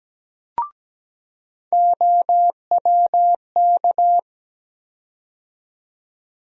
Morse code, heard as a keyed tone, about 17 wpm, 700 Hz. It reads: OWK
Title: USA Call Sign Postfixes - 3 Letters 17wpm